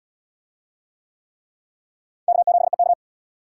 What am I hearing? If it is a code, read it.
Morse code, heard as a keyed tone, about 67 wpm, 700 Hz. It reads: K7EK